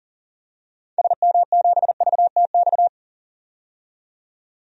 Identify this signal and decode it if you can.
Morse code, heard as a keyed tone, about 40 wpm, 700 Hz. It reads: SM7VTX
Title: DXCC-100 Call Signs - 6 Characters - 40wpm